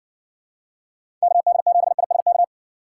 Morse code, heard as a keyed tone, about 59 wpm, 700 Hz. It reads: KD7ISK